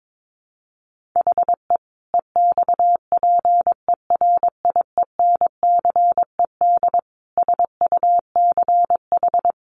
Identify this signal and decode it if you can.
Morse code, heard as a keyed tone, about 22 wpm, 700 Hz. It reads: HE EXPERIENCED SUCH
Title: Sentences from Top 1100 Words 22wpm